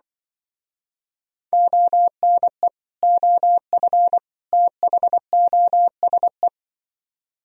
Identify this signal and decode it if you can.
Morse code, heard as a keyed tone, about 24 wpm, 700 Hz. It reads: ONE OF THOSE
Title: Speed-Racing - Sets of 3 Words 20wpm